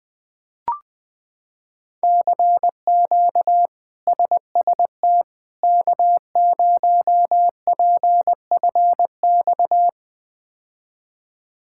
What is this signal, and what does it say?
Morse code, heard as a keyed tone, about 21 wpm, 700 Hz. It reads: CQ SST K0PFX
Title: K1USN SST QSO Practice - 20wpm